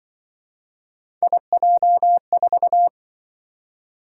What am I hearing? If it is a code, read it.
Morse code, heard as a keyed tone, about 25 wpm, 700 Hz. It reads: IJ4
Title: Speed-Racing - Three Letter-Number Combinations - 20wpm